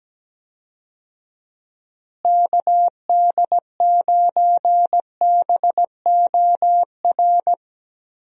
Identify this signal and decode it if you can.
Morse code, heard as a keyed tone, about 17 wpm, 700 Hz. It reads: KD9BOR
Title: USA Call Signs - 6 Characters - Repeated 3x - 17wpm